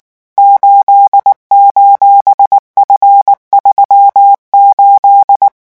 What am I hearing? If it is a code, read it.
Morse code, heard as a keyed tone, about 19 wpm, 800 Hz. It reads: 8:F38
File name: morse mp3 / revision14-vitesse 18 mots